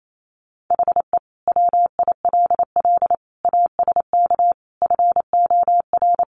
Morse code, heard as a keyed tone, about 28 wpm, 700 Hz. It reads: HE WILL ASK FOR
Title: Sentences from Top 200 Words 28wpm